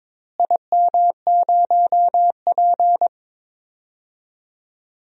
Morse code, heard as a keyed tone, about 22 wpm, 700 Hz. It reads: IM0P
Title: DXCC-100 Call Signs - 4 Characters - Repeated 3x - 22wpm